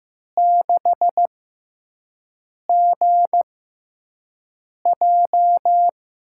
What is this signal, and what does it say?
Morse code, heard as a keyed tone, about 15 wpm, spaced as for 5 wpm, 700 Hz. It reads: 6GJ
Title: USA Call Signs - 4 Characters - 6x Character Spacing - 15wpm